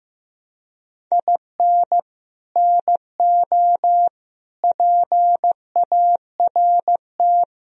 Morse code, heard as a keyed tone, about 16 wpm, 700 Hz. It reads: IN NO PART
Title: Sentences from Top 1000 Words - Review Part 3 - 15wpm